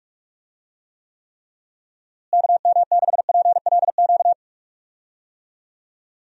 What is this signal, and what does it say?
Morse code, heard as a keyed tone, about 42 wpm, 700 Hz. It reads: KM6PLQ
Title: USA Call Signs - 6 Characters - Repeated 3x - 45wpm